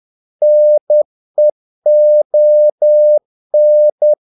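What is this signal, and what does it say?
Morse code, heard as a keyed tone, about 10 wpm, 600 Hz. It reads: NEON
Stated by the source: Morse practice of short Danish words at 10 wpm, from CW med Gnister.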